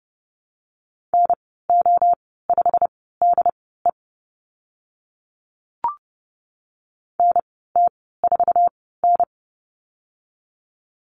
Morse code, heard as a keyed tone, about 30 wpm, spaced as for 17 wpm, 700 Hz. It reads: NO5DE NT4N
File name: Lesson 15g - Number 5 - Call Signs 30wpm